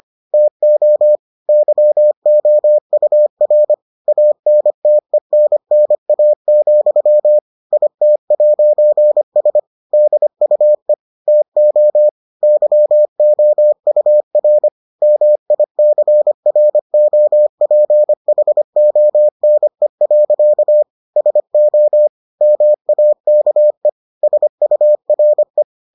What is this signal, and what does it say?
Morse code, heard as a keyed tone, about 25 wpm, 600 Hz. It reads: TO YOUR ANTENNA, IT'S DUE TO YOUR MICROPHONE. SO MAKE SURE